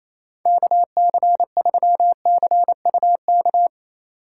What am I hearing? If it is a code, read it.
Morse code, heard as a keyed tone, about 34 wpm, 700 Hz. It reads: KC3CUK
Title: Speed Racing - USA 2x3 Call Signs 28wpm